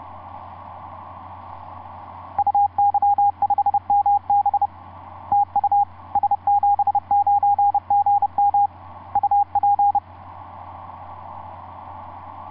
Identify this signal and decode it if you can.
Morse code, heard as a keyed tone, about 30 wpm, 810 Hz, with radio noise. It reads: UY5MB TU S79GM UP
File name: s79gm on4on etc 15m cw 30mar2010 0653